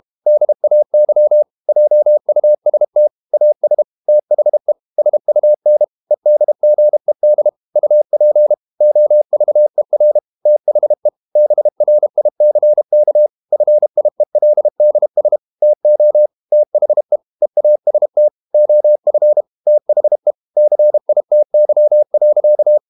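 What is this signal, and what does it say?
Morse code, heard as a keyed tone, about 32 wpm, 600 Hz. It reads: DAY JUST AS THE SUN EDGED UP OVER THE BRICK FIELDS TO THE EAST OF THE CITY.
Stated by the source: Morse code practice at 32 wpm from CW med Gnister, reading Ken Follett's novel Never in English.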